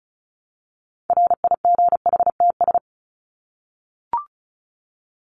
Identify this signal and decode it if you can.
Morse code, heard as a keyed tone, about 34 wpm, 700 Hz. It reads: RIGHTS
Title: ES:S Suffix - Top 100 Words with Base Words 35wpm